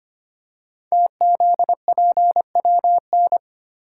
Morse code, heard as a keyed tone, about 25 wpm, 700 Hz. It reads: TZPWN
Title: Code Groups - 5 Letter:Numbers - 25wpm